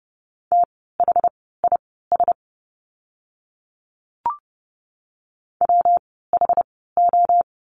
Morse code, heard as a keyed tone, about 30 wpm, spaced as for 17 wpm, 700 Hz. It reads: THIS WHO
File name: course / 30:10wpm / 237 Lesson 38f - Letter q - Words 30wpm